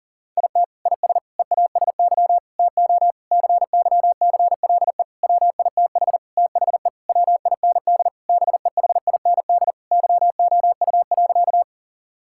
Morse code, heard as a keyed tone, about 40 wpm, 700 Hz. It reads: IT IS EASY TO CYCLE WITH THE WIND BEHIND YOU.
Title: Sentences from Top 1000 Words - Review Part 6 - 40wpm